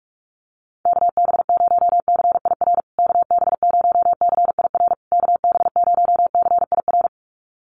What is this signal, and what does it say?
Morse code, heard as a keyed tone, about 44 wpm, 700 Hz. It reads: KB0CIR KB0CIR KB0CIR
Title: USA Call Signs Repeated 3x 45wpm